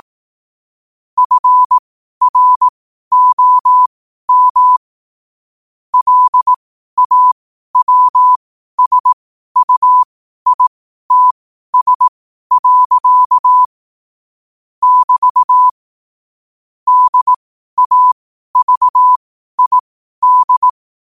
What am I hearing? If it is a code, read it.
Morse code, heard as a keyed tone, about 18 wpm, spaced as for 12 wpm, 1000 Hz. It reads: FROM LAWSUITS. <BT> DAVID
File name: qotdfiles / 12wpm